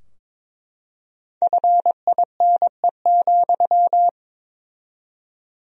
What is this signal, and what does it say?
Morse code, heard as a keyed tone, about 22 wpm, 700 Hz. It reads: FINE,
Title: 4x Word-Spacing - Sentences from Top 500 Words - 22wpm